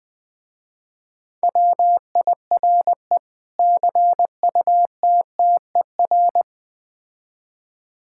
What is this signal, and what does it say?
Morse code, heard as a keyed tone, about 20 wpm, 700 Hz. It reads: WIRE CUTTER
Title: Difficult Words with Hints - 20wpm